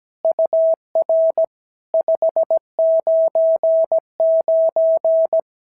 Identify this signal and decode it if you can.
Morse code, heard as a keyed tone, about 16 wpm, 650 Hz. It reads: UR 599
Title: SOTA QSO Practice 17wpm